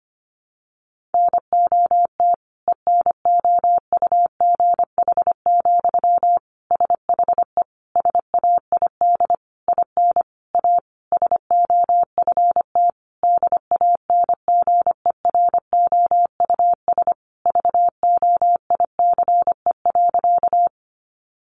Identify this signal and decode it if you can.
Morse code, heard as a keyed tone, about 25 wpm, 700 Hz. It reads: NOT ENOUGH, SHE SAID IN A SOFT DANGEROUS VOICE.
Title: Sentences from Top 1500 Words 25wpm